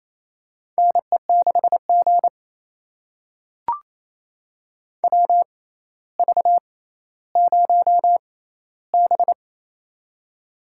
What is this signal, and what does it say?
Morse code, heard as a keyed tone, about 28 wpm, spaced as for 14 wpm, 700 Hz. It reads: NE6G WV0B
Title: USA Call Signs - 4 Characters - 6x Character Spacing - 28wpm